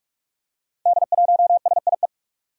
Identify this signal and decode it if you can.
Morse code, heard as a keyed tone, about 44 wpm, 700 Hz. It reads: D1SIE
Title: Five Challenging Letter-Number Combinations 45wpm